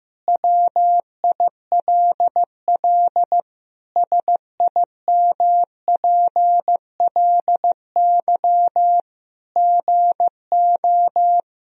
Morse code, heard as a keyed tone, about 16 wpm, 700 Hz. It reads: WILL SIMPLY GO
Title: Sentences from Top 1000 Words - Review Part 4 - 15wpm